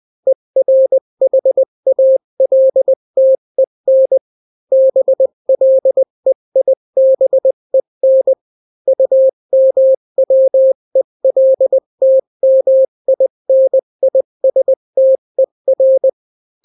Morse code, heard as a keyed tone, about 20 wpm, 530 Hz. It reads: ERHALTEN BLEIBEN UMWELTMINISTER
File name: daily / 18wpm